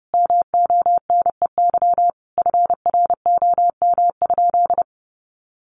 Morse code, heard as a keyed tone, about 30 wpm, 700 Hz. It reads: MONEY FROM?
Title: Sentences from Top 400 Words 30wpm